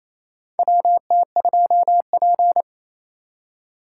Morse code, heard as a keyed tone, about 28 wpm, 700 Hz. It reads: WT2P
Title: USA Call Signs - 4 Characters - 3x Character Spacing - 28wpm